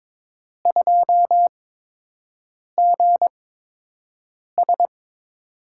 Morse code, heard as a keyed tone, about 22 wpm, spaced as for 6 wpm, 700 Hz. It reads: 2GS
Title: USA Call Signs - 4 Characters - 8x Character Spacing - 22wpm